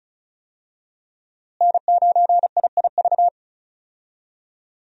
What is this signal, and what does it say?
Morse code, heard as a keyed tone, about 34 wpm, 700 Hz. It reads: N9IIV
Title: USA Call Signs - 5 Characters - Repeated 2x - 35wpm